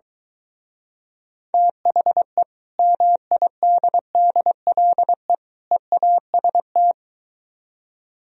Morse code, heard as a keyed tone, about 23 wpm, 700 Hz. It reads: THE MIDDLE EAST